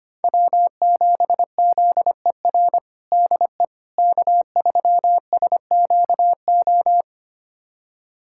Morse code, heard as a keyed tone, about 25 wpm, 700 Hz. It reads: W7ZER DE K3SQO
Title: USA Callsign DE Callsign - 5 Characters - 25wpm